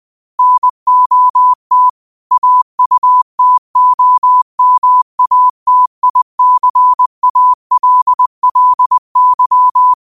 Morse code, heard as a keyed tone, about 20 wpm, 1000 Hz. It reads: NOT AUTOMATICALLY
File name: qotdfiles / 20wpm